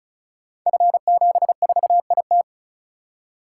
Morse code, heard as a keyed tone, about 36 wpm, 690 Hz. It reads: F74IT
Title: Code Groups - 5 Letter:Numbers - 35wpm